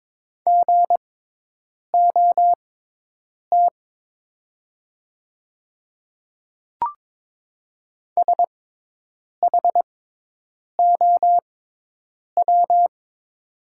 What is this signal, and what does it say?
Morse code, heard as a keyed tone, about 22 wpm, spaced as for 8 wpm, 700 Hz. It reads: GOT SHOW
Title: 6x Character Spacing - Top 300 Words - 22wpm